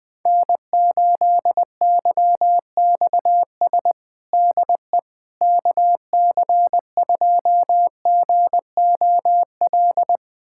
Morse code, heard as a keyed tone, about 20 wpm, 700 Hz. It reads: N8YXS DE KC2GOL
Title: USA Callsign DE Callsign 20wpm